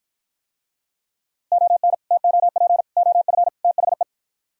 Morse code, heard as a keyed tone, about 52 wpm, 700 Hz. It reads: ON TOP OF THE